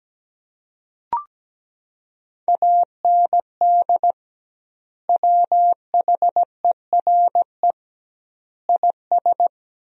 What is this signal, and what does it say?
Morse code, heard as a keyed tone, about 17 wpm, 700 Hz. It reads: AND WHERE IS
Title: 2x Word-Spacing - Sentences from Top 200 Words - 17wpm